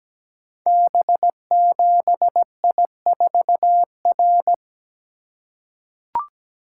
Morse code, heard as a keyed tone, about 17 wpm, 700 Hz. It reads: B7I4R
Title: Code Groups - 5 Letter:Numbers - 17wpm